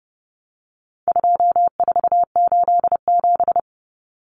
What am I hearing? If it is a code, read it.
Morse code, heard as a keyed tone, about 30 wpm, 700 Hz. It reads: 2487
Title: Numbers - 4-Digit 30wpm